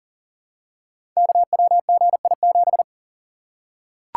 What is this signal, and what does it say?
Morse code, heard as a keyed tone, about 41 wpm, 700 Hz. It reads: KWGI7
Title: Code Groups - 5 Letter:Numbers - 40wpm